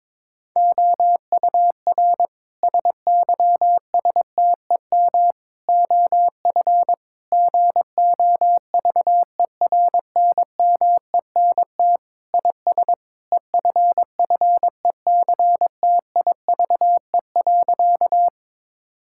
Morse code, heard as a keyed tone, about 21 wpm, 700 Hz. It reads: OUR SYSTEM OF GOVERNMENT IS EFFECTIVE.
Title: Sentences from Top 1000 Words - Review Part 6 - 22wpm